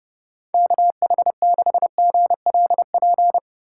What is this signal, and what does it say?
Morse code, heard as a keyed tone, about 37 wpm, 700 Hz. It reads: KH6GLP
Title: Speed Racing - USA 2x3 Call Signs 30wpm